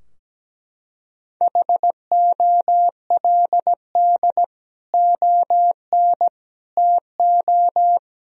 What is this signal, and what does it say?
Morse code, heard as a keyed tone, about 17 wpm, 700 Hz. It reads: HOLD ON TO
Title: Sentences from Top 1000 Words - Review Part 2 - 17wpm